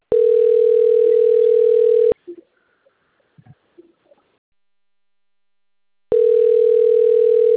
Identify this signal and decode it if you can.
Ringback tone (North America), 440 + 480 Hz.
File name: out- - - - - .7054